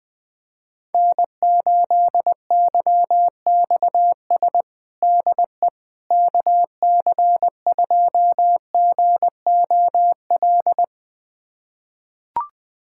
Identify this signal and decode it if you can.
Morse code, heard as a keyed tone, about 21 wpm, 700 Hz. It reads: N8YXS DE KC2GOL
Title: USA Callsign DE Callsign 20wpm